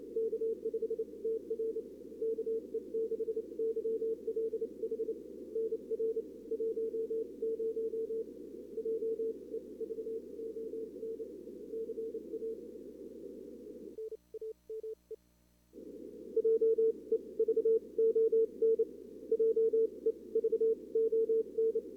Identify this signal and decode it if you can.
Morse code, heard as a keyed tone, about 29 wpm, spaced as for 24 wpm, 440 Hz. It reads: K5TR KE6YLH NR 10 JEVON CA NAME JEVON JEVON
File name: CWSprint / Feb2022 / k5tr-feb-na-sprint-cw-k5ot -0000